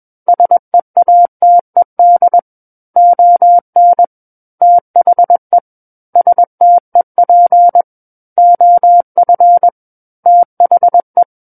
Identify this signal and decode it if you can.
Morse code, heard as a keyed tone, about 21 wpm, 700 Hz. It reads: SEATED ON THE STEP OF THE
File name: Book One / 12-Chapter 12 - 21 WPM